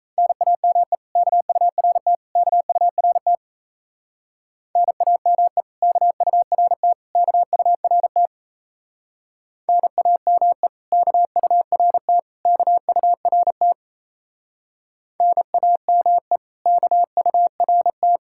Morse code, heard as a keyed tone, about 36 wpm, 700 Hz. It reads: NAME KURT KURT NAME KURT KURT NAME KURT KURT NAME KURT